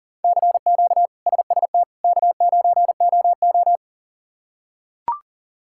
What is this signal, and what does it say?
Morse code, heard as a keyed tone, about 40 wpm, 700 Hz. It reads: CQ SST K9OO
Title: K1USN SST QSO Practice - 40wpm